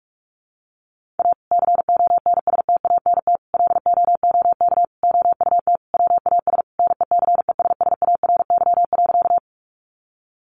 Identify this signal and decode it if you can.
Morse code, heard as a keyed tone, about 44 wpm, 700 Hz. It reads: A CONSTANT LOOK OUT WAS NECESSARY.
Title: Sentences from Top 1500 Words 45wpm